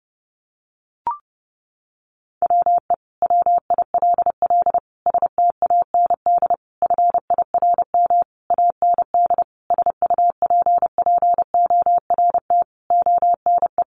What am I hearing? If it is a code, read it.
Morse code, heard as a keyed tone, about 30 wpm, 700 Hz. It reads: WE WILL STAND FIRM AND SUPPORT ONE
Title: Sentences from Top 1400 Words 30wpm